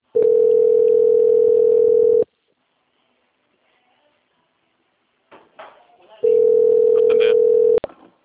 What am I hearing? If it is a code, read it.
Ringback tone (North America), 440 + 480 Hz, with speech.
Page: Dracena 2 - SP